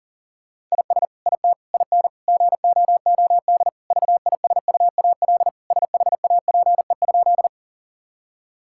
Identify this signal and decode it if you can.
Morse code, heard as a keyed tone, about 40 wpm, 700 Hz. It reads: IS IT IN GOOD VISUAL SHAPE?
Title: Sentences from Top 1000 Words - Review Part 6 - 40wpm